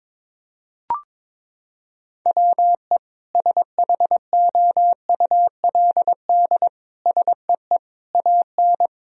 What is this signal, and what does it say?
Morse code, heard as a keyed tone, about 22 wpm, 700 Hz. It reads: WE SHOULD SEE AN
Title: Sentences from Top 1000 Words - Review Part 4 - 22wpm